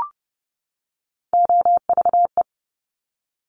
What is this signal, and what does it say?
Morse code, heard as a keyed tone, about 30 wpm, 700 Hz. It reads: OVE
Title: Top Three Letter Combinations in Words 30wpm